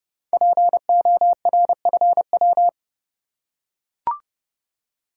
Morse code, heard as a keyed tone, about 30 wpm, 700 Hz. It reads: PORFW
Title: Code Groups - 5 Letters - 30wpm